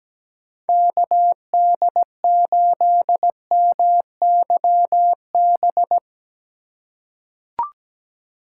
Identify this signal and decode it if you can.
Morse code, heard as a keyed tone, about 17 wpm, 700 Hz. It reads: KD8MYB
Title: USA Call Signs - 6 Characters - 2x Character Spacing - 17wpm